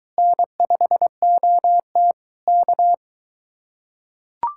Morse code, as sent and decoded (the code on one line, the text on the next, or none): -. ..... --- - / -.-
N5OT K